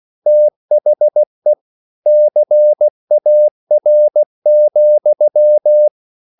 Morse code, heard as a keyed tone, about 16 wpm, 600 Hz. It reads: THE CAR,